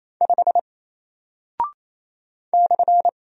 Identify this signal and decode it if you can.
Morse code, heard as a keyed tone, about 28 wpm, 700 Hz. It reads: 5 /
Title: Single Character (Letter:Number:Punctuation:Prosign) Alt 28wpm